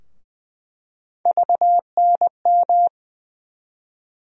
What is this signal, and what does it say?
Morse code, heard as a keyed tone, about 20 wpm, 700 Hz. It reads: VNM
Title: USA Call Sign Postfixes - 3 Letters 20wpm